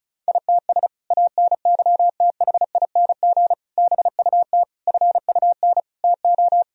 Morse code, heard as a keyed tone, about 36 wpm, 700 Hz. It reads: ITS ANYTHING BUT FUN TO
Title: Sentences from Top 1000 Words - Review Part 4 - 35wpm